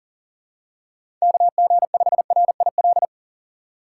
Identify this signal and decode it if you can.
Morse code, heard as a keyed tone, about 40 wpm, 700 Hz. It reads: KG5RIL
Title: USA Call Signs - 6 Characters - 6x Character Spacing - 40wpm